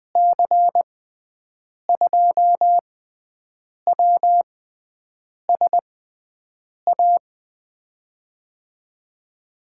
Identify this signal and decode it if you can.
Morse code, heard as a keyed tone, about 20 wpm, spaced as for 7 wpm, 700 Hz. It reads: C2WSA
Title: USA Call Signs - 6x Character Spacing - 20wpm